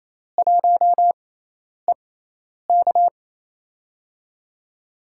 Morse code, heard as a keyed tone, about 28 wpm, spaced as for 10 wpm, 700 Hz. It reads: 1EK
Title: USA Call Signs - 5 Characters - 6x Character Spacing - 28wpm